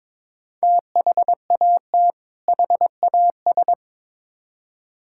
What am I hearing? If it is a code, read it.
Morse code, heard as a keyed tone, about 22 wpm, 700 Hz. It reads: THAT HAS
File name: Build A Phrase - 2-Word Phrases - 22wpm